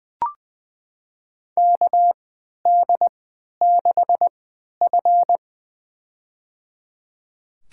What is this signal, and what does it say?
Morse code, heard as a keyed tone, about 21 wpm, spaced as for 12 wpm, 700 Hz. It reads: KD6F
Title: USA Call Signs - 4 Characters - 3x Character Spacing - 20wpm